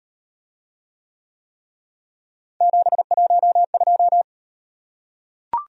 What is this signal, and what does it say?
Morse code, heard as a keyed tone, about 38 wpm, 700 Hz. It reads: 712